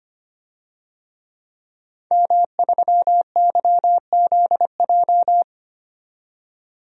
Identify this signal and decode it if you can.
Morse code, heard as a keyed tone, about 25 wpm, 700 Hz. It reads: M3YZJ
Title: DXCC-100 Call Signs - 5 Characters - 25wpm